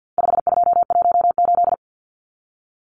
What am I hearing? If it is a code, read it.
Morse code, heard as a keyed tone, about 50 wpm, 700 Hz. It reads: 5218
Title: Numbers - 4-Digit 50wpm